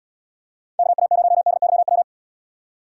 Morse code, heard as a keyed tone, about 77 wpm, 700 Hz. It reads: KA0DYW